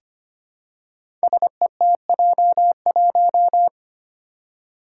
Morse code, heard as a keyed tone, about 25 wpm, 700 Hz. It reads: SETJ1
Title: Five Challenging Letter-Number Combinations 25wpm